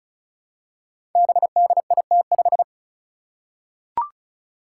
Morse code, heard as a keyed tone, about 36 wpm, 700 Hz. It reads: BDIT5
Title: Five Challenging Letter-Number Combinations 35wpm